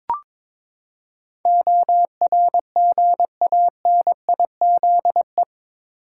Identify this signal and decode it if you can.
Morse code, heard as a keyed tone, about 22 wpm, 700 Hz. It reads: ORGANIZE
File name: practice / 22wpm / Eight Letter Words - 22wpm